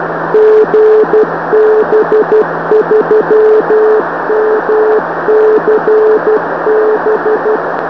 Morse code, heard as a keyed tone, about 12 wpm, 430 Hz, with noise. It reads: GB3MCB
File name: gb3mcb-4m